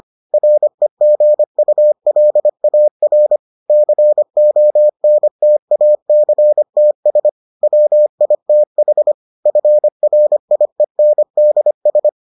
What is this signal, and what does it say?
Morse code, heard as a keyed tone, about 25 wpm, 600 Hz. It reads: REGULAR CONTACTS WITH FRIENDS